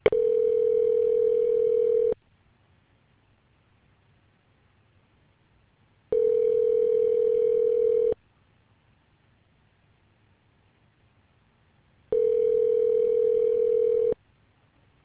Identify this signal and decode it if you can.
Ringback tone (North America), 440 + 480 Hz.